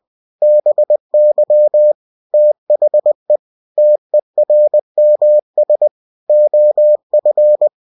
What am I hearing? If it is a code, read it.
Morse code, heard as a keyed tone, about 20 wpm, 600 Hz. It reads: BY THE TERMS OF
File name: cw-books / winnie-the-pooh / winnie-20wpm-page-0113